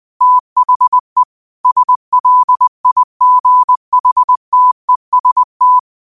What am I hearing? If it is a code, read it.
Morse code, heard as a keyed tone, about 20 wpm, 1000 Hz. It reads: THE SLIGHTEST